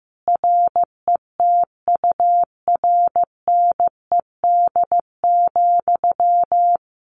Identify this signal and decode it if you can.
Morse code, heard as a keyed tone, about 15 wpm, 700 Hz. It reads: RETURNED,
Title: A Sound Of Thunder 15wpm